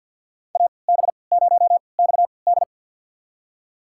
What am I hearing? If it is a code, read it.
Morse code, heard as a keyed tone, about 51 wpm, spaced as for 28 wpm, 700 Hz. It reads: AB0XD
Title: USA Call Signs - 3x Character Spacing - 50wpm